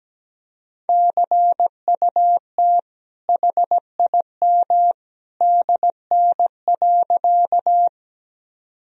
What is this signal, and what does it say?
Morse code, heard as a keyed tone, about 17 wpm, 700 Hz. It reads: CUT HIM DN.
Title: Sentences from Top 300 Words and Abbreviations 17wpm